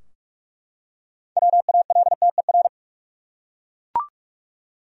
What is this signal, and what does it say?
Morse code, heard as a keyed tone, about 42 wpm, 700 Hz. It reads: WALTER
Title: USA Common Names - 45wpm